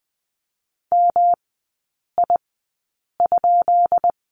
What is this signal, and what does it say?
Morse code, heard as a keyed tone, about 20 wpm, spaced as for 8 wpm, 700 Hz. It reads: MI?
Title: 212 Lesson 34e - Number 8 - Three Characters 20wpm